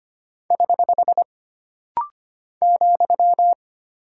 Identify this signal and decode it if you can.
Morse code, heard as a keyed tone, about 25 wpm, 700 Hz. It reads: <HH> ,
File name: Single Punctuation-Prosign - Rapid-Fire - 25wpm